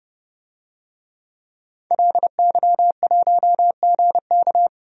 Morse code, heard as a keyed tone, about 30 wpm, 700 Hz. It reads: LY1GK
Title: DXCC-100 Call Signs - 5 Characters - 30wpm